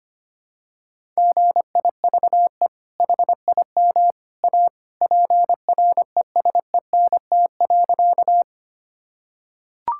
Morse code, heard as a keyed tone, about 25 wpm, 700 Hz. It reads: GIVE HIM A PRESENT.
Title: Sentences from Top 1000 Words - Review Part 5 - 25wpm